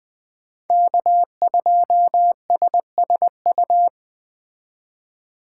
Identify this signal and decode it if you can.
Morse code, heard as a keyed tone, about 20 wpm, 700 Hz. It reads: K2SSU